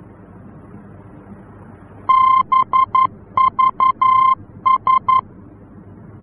Morse code, heard as a keyed tone, about 11 wpm, 1050 Hz. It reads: BVS
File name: france / mx / BVS-F-VOR